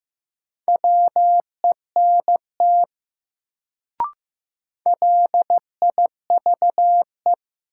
Morse code, heard as a keyed tone, about 16 wpm, 700 Hz. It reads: WENT LIVE
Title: Sentences from Top 200 Words - New Word Drill - 15wpm